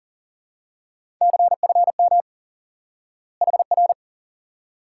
Morse code, heard as a keyed tone, about 40 wpm, 700 Hz. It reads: CFM HR
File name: course / instant-qso-element-course-3 / IQE Course 3 - Lesson 89 - YAGI - Two Elements - 40wpm